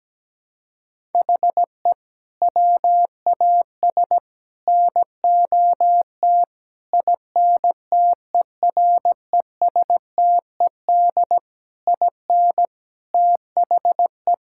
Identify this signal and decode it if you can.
Morse code, heard as a keyed tone, about 17 wpm, 700 Hz. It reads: HE WAS NOT INTERESTED IN THE
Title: Sentences from Top 1000 Words - Review Part 6 - 17wpm